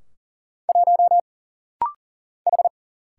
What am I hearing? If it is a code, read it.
Morse code, heard as a keyed tone, about 39 wpm, 700 Hz. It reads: 1 H
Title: Single Letter-Number Alt - Rapid-Fire 40wpm